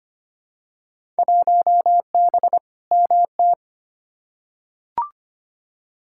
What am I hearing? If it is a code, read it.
Morse code, heard as a keyed tone, about 25 wpm, 700 Hz. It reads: 1B MT
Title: Field Day Exchanges - 25wpm